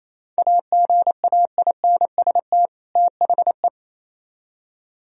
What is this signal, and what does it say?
Morse code, heard as a keyed tone, about 35 wpm, 700 Hz. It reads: AGAINST THE
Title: Speed-Racing - Sets of 2 Words 28wpm